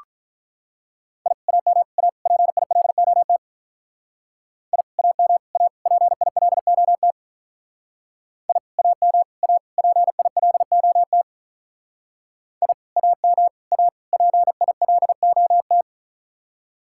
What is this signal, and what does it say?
Morse code, heard as a keyed tone, about 42 wpm, 700 Hz. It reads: I AM A PILOT I AM A PILOT I AM A PILOT I AM A PILOT
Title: Speed-Racing Common QSO Phrases - 35wpm